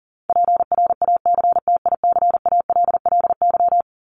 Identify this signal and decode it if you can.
Morse code, heard as a keyed tone, about 40 wpm, 700 Hz. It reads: PRACTICALLY
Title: LY Suffix - Top 100 Words with Base Words 40wpm